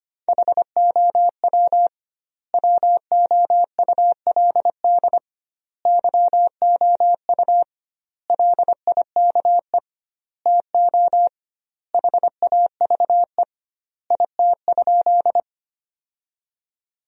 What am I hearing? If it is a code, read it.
Morse code, heard as a keyed tone, about 25 wpm, 700 Hz. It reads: HOW WOULD YOU LIKE TO HAVE IT?
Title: 2x Word-Spacing - Sentences from Top 100 Words - 25wpm